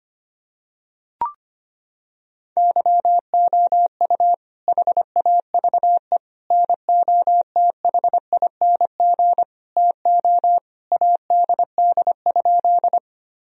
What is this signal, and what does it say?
Morse code, heard as a keyed tone, about 25 wpm, 700 Hz. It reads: YOU HAVE NOTHING TO ADD?
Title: Sentences from Top 1000 Words - Review Part 1 - 25wpm